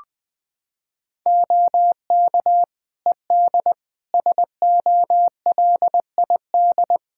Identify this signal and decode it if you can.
Morse code, heard as a keyed tone, about 20 wpm, 700 Hz. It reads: OK ED SOLID
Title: Practice QSO - Exchange 3 - More Info - 20wpm